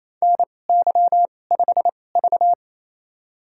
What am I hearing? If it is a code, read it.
Morse code, heard as a keyed tone, about 28 wpm, spaced as for 20 wpm, 700 Hz. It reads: NY5V
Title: USA Call Signs - 4 Characters - 2x Character Spacing - 28wpm